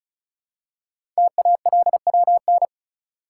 Morse code, heard as a keyed tone, about 36 wpm, 690 Hz. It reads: TALWN
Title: Code Groups - 5 Letters - 35wpm